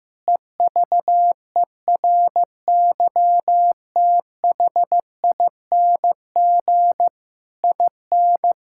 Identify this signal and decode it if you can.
Morse code, heard as a keyed tone, about 16 wpm, 700 Hz. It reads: EVERYTHING IN
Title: Sentences from Top 1000 Words - Review Part 6 - 15wpm